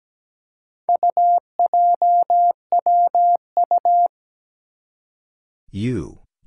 Morse code, heard as a keyed tone, about 17 wpm, 700 Hz. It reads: UJWU
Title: Four Challenging Letter-Number Combinations 17wpm